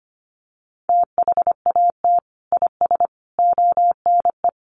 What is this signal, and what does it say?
Morse code, heard as a keyed tone, about 25 wpm, 700 Hz. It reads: THAT IS ONE
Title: Sentences from Top 1200 Words 25wpm